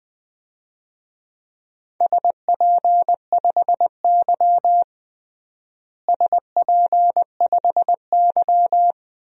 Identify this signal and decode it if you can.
Morse code, heard as a keyed tone, about 20 wpm, 700 Hz. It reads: SP5Y SP5Y
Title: DXCC-100 Call Signs - 4 Characters - Repeated 2x - 20wpm